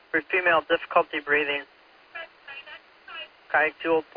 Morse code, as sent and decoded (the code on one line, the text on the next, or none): none